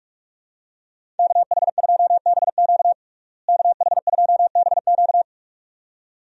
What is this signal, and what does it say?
Morse code, heard as a keyed tone, about 46 wpm, 700 Hz. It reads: KH2BQ KH2BQ
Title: USA Call Signs - 5 Characters - Repeated 2x - 45wpm